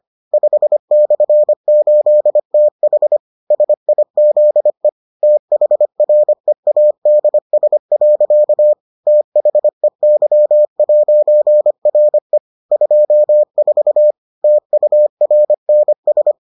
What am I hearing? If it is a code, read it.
Morse code, heard as a keyed tone, about 25 wpm, 600 Hz. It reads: 5/8TH SIZE THREADS. THEY'RE 24 TURNS